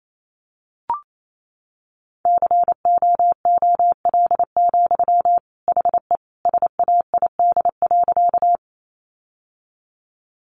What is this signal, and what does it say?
Morse code, heard as a keyed tone, about 28 wpm, 700 Hz. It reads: COOL, HE SAID.
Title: Sentences from Top 1200 Words 28wpm